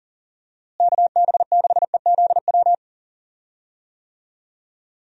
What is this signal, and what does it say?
Morse code, heard as a keyed tone, about 40 wpm, 700 Hz. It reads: KB6EZW
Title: USA Call Signs - 6 Characters - Repeated 2x - 40wpm